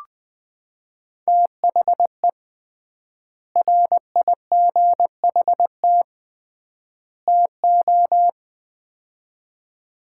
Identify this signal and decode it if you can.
Morse code, heard as a keyed tone, about 20 wpm, 700 Hz. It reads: THE RIGHT TO
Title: 3x Word-Spacing - Sets of 3 Words - 20wpm